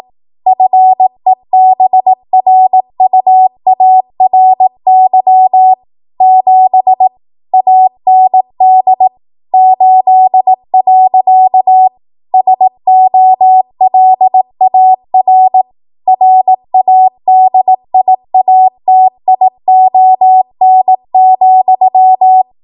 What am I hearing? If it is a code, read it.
Morse code, heard as a keyed tone, about 18 wpm, 750 Hz. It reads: FEBRUARY 7 AND 8. SOLAR RADIATION,
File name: Morse / 18WPM